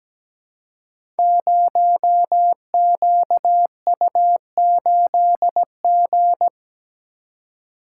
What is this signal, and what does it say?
Morse code, heard as a keyed tone, about 17 wpm, 700 Hz. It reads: 0QU8G
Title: Code Groups - 5 Letter:Numbers - 17wpm